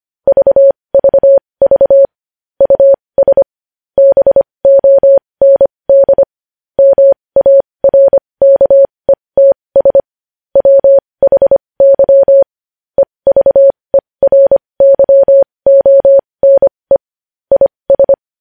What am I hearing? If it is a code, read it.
Morse code, heard as a keyed tone, about 25 wpm, spaced as for 20 wpm, 550 Hz. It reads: VVV US BOND MARKETS WHY EVERYONE IS
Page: news-headlines-in-morse-code-at-20-wpm